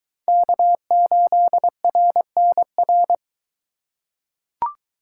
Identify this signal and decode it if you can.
Morse code, heard as a keyed tone, about 22 wpm, 700 Hz. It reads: K8RNR